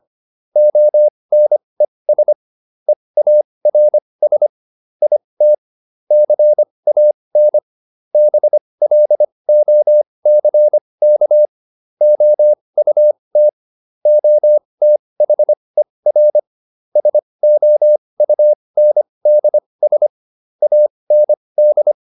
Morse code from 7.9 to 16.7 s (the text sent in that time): BLOCK OUT OTHER